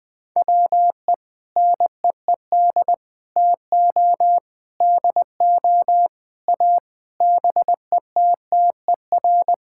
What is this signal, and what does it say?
Morse code, heard as a keyed tone, about 20 wpm, 700 Hz. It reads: WE NEED TO DO A BETTER
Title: Sentences with 6-Word Phrases - Spoken Beginning - 20wpm